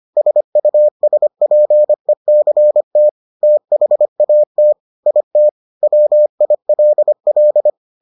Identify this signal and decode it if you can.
Morse code, heard as a keyed tone, about 25 wpm, 600 Hz. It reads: SUSPECT THAT IT WILL